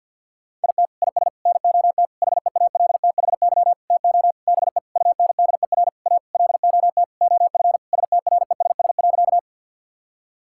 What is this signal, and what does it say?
Morse code, heard as a keyed tone, about 50 wpm, 700 Hz. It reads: IT IS NOT HEALTHY TO BE UNDER A LOT OF STRESS.
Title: Sentences from Top 1000 Words - Review Part 6 - 50wpm